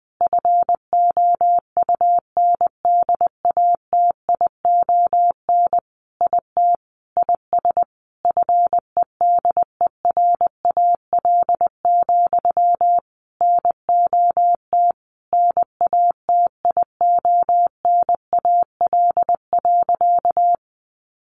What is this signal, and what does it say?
Morse code, heard as a keyed tone, about 20 wpm, 700 Hz. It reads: FOUNDATION IT IS FEDERAL, NOT NATIONAL.
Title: Sentences from Top 1200 Words 20wpm